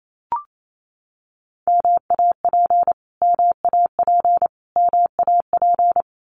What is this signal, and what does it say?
Morse code, heard as a keyed tone, about 28 wpm, 700 Hz. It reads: MAP MAP MAP
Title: Top 500 Words Repeated 28wpm